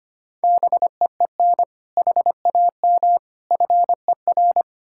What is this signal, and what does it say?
Morse code, heard as a keyed tone, about 33 wpm, 700 Hz. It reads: BEEN HAM FER